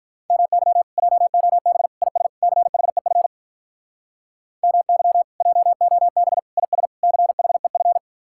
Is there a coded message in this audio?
Yes: Morse code from 0.3 to 8.0 s, heard as a keyed tone, about 45 wpm, 700 Hz. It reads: MY JOB IS CHEF MY JOB IS CHEF